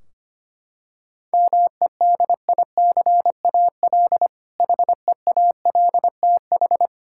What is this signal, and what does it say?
Morse code, heard as a keyed tone, about 25 wpm, 700 Hz. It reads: MEDICAL HEALTH
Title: K1USN SST QSO Practice 25wpm